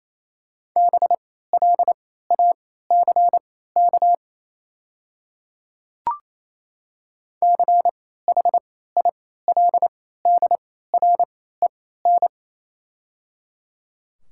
Morse code, heard as a keyed tone, about 28 wpm, spaced as for 16 wpm, 700 Hz. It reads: BLACK CHILDREN
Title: 3x Character Spacing - Top 500 Words - 28wpm